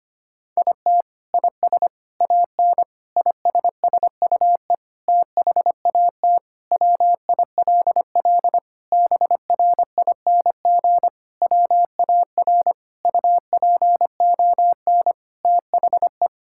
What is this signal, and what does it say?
Morse code, heard as a keyed tone, about 25 wpm, 700 Hz. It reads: IT IS AN ISSUE THAT WILL BRING WAR UPON THE
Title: Sentences from Top 1000 Words - Review Part 6 - 25wpm